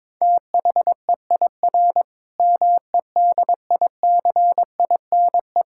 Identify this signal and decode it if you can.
Morse code, heard as a keyed tone, about 22 wpm, 700 Hz. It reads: THEIR MEDICINE
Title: Sentences from Top 1000 Words - Review Part 6 - 22wpm